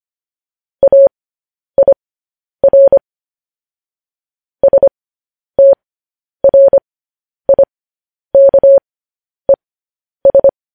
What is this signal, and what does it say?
Morse code, heard as a keyed tone, about 25 wpm, spaced as for 10 wpm, 550 Hz. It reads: AIR STRIKES
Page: news-headlines-in-morse-code-at-10-wpm